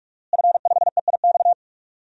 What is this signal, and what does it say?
Morse code, heard as a keyed tone, about 46 wpm, 700 Hz. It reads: F5EIX
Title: DXCC-100 Call Signs - 5 Characters - Repeated 2x - 45wpm